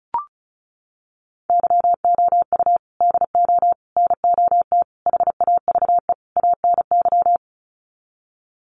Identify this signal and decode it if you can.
Morse code, heard as a keyed tone, about 34 wpm, 700 Hz. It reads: YOU DO NOT HAVE ANY
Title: Sets of 5 Words Encore 35wpm